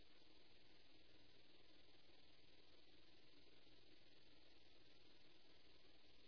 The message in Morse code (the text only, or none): none